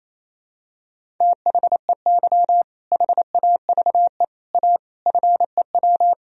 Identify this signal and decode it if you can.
Morse code, heard as a keyed tone, about 28 wpm, 700 Hz. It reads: THEY HAVE A FEW
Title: Sentences from Top 1000 Words - Review Part 1 - 28wpm